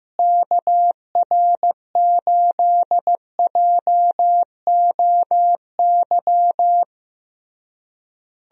Morse code, heard as a keyed tone, about 15 wpm, 700 Hz. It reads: KR8JOY
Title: USA Call Signs - 6 Characters - Repeated 3x - 15wpm